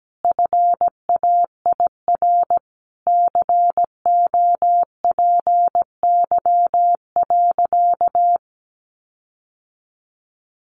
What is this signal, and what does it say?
Morse code, heard as a keyed tone, about 17 wpm, 700 Hz. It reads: FAIR COPY.
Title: Sentences from Top 1400 Words 17wpm